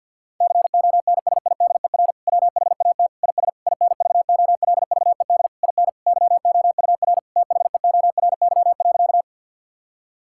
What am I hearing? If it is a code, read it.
Morse code, heard as a keyed tone, about 50 wpm, 700 Hz. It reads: CONSIDER WHAT IS INVOLVED IN YOUR THEORY.
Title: Sentences from Top 1000 Words - Review Part 5 - 50wpm